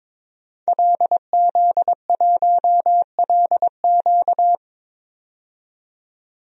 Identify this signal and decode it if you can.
Morse code, heard as a keyed tone, about 22 wpm, 700 Hz. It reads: LZ1LQ
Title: DXCC-100 Call Signs - 5 Characters - 22wpm